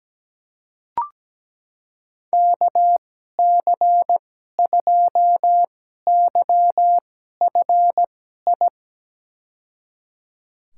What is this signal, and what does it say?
Morse code, heard as a keyed tone, about 17 wpm, spaced as for 12 wpm, 700 Hz. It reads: KC2YFI
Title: USA Call Signs - 6 Characters - 2x Character Spacing - 17wpm